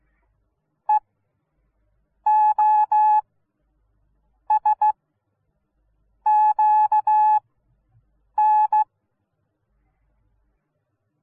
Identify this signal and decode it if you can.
Morse code, heard as a keyed tone, about 16 wpm, spaced as for 6 wpm, 850 Hz. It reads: EOSQN